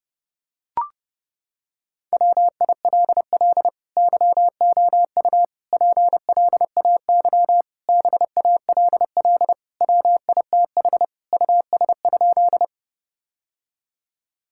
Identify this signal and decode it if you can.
Morse code, heard as a keyed tone, about 32 wpm, 700 Hz. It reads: WILL YOU PLAY BALL WITH US?
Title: Sentences from Top 1000 Words - Review Part 6 - 30wpm